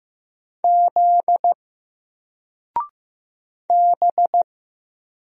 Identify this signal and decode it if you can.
Morse code, heard as a keyed tone, about 15 wpm, 700 Hz. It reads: Z B
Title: Single Letter-Number 15wpm